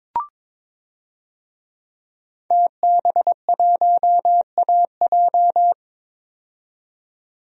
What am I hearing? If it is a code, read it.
Morse code, heard as a keyed tone, about 22 wpm, 700 Hz. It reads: TB1AJ
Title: DXCC-100 Call Signs - 5 Characters - 22wpm